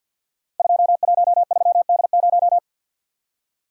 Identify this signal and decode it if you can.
Morse code, heard as a keyed tone, about 53 wpm, 700 Hz. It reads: 213D0
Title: Code Groups - 5 Letter:Numbers - 50wpm